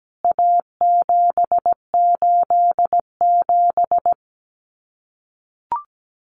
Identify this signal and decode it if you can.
Morse code, heard as a keyed tone, about 17 wpm, 700 Hz. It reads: A787
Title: CWT Exchange 17wpm